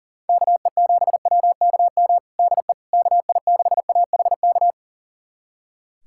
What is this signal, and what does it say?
Morse code, heard as a keyed tone, about 40 wpm, 700 Hz. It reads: KE7WKM DE KI6AHK
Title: USA Callsign DE Callsign - 6 Characters - 40wpm